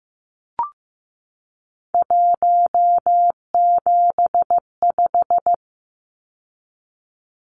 Morse code, heard as a keyed tone, about 15 wpm, 700 Hz. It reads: 175
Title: Numbers - 3-Digit 15wpm